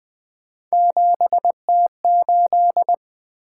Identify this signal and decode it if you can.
Morse code, heard as a keyed tone, about 20 wpm, 700 Hz. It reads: 7T8
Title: Three Letter-Number Combinations - 20wpm